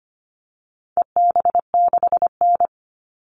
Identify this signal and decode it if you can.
Morse code, heard as a keyed tone, about 25 wpm, 700 Hz. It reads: EB6N
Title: All Cut Numbers - 4-Digit 25wpm